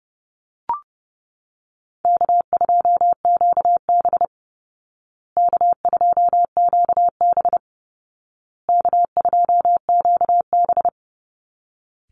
Morse code, heard as a keyed tone, about 30 wpm, 700 Hz. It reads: K2QB K2QB K2QB
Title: Call Signs Repeated 2x - 30wpm